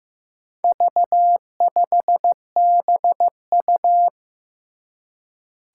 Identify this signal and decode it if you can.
Morse code, heard as a keyed tone, about 15 wpm, 700 Hz. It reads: V5BU
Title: Four Challenging Letter-Number Combinations 15wpm